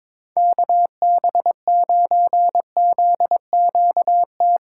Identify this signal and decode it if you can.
Morse code, heard as a keyed tone, about 21 wpm, 700 Hz. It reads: KB9ZQT
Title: USA Callsign DE Callsign 22wpm